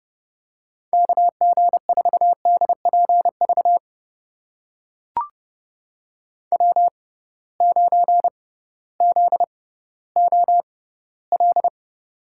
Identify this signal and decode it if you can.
Morse code, heard as a keyed tone, about 32 wpm, spaced as for 16 wpm, 700 Hz. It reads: KG4DPV W9ZOL
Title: USA Call Signs - 6x Character Spacing - 30wpm